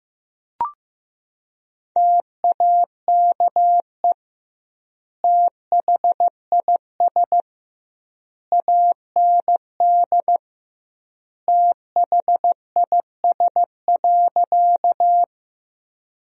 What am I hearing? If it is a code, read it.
Morse code, heard as a keyed tone, about 15 wpm, 700 Hz. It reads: TAKE THIS AND THIS.
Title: 2x Word-Spacing - Sentences from Top 200 Words - 15wpm